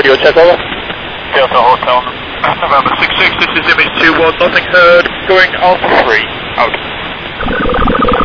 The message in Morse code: none

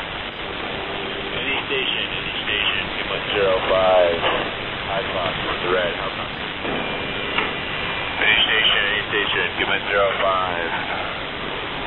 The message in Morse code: none

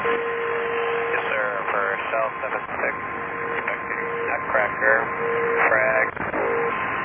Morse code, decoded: none

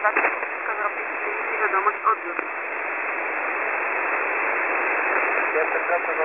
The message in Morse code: none

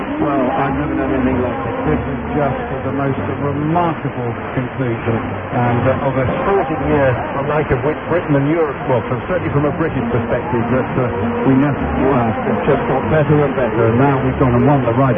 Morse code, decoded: none

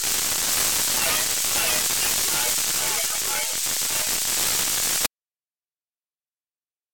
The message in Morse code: none

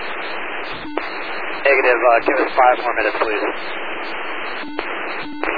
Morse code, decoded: none

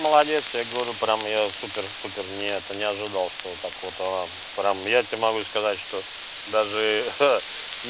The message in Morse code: none

